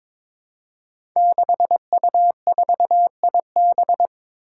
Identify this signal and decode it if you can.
Morse code, heard as a keyed tone, about 22 wpm, 700 Hz. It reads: 6U4IB
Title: Five Challenging Letter-Number Combinations 22wpm